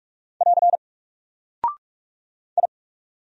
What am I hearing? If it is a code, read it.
Morse code, heard as a keyed tone, about 44 wpm, 700 Hz. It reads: <AR> I
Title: Single Character (Letter:Number:Punctuation:Prosign) 45wpm